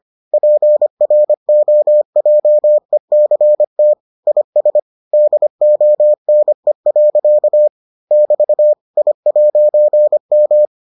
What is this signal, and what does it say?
Morse code, heard as a keyed tone, about 25 wpm, 600 Hz. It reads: PROJECT IS DONE. <BT> I'M